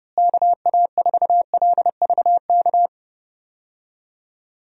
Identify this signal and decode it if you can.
Morse code, heard as a keyed tone, about 30 wpm, 700 Hz. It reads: KA4LVK
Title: USA Call Signs - 6 Characters - Repeated 3x - 30wpm